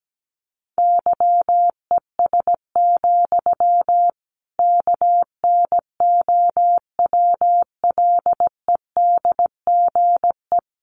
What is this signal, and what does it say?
Morse code, heard as a keyed tone, about 17 wpm, 700 Hz. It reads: YES, KNOWLEDGE
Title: Sentences from Top 1500 Words 17wpm